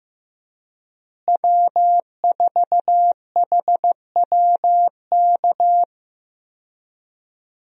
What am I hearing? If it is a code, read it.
Morse code, heard as a keyed tone, about 15 wpm, 700 Hz. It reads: W4HWK
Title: USA Call Signs - 5 Characters - Repeated 2x - 15wpm